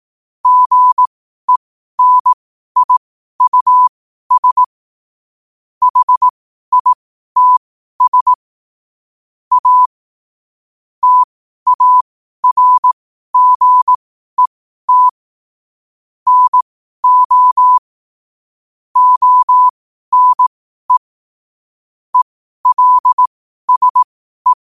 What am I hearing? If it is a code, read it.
Morse code, heard as a keyed tone, about 18 wpm, spaced as for 12 wpm, 1000 Hz. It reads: GENIUS HITS A TARGET NO ONE ELSE
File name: qotdfiles / 12wpm